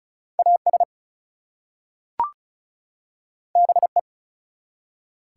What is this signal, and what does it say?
Morse code, heard as a keyed tone, about 36 wpm, 700 Hz. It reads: AS BE
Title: Two Letter Words - 35wpm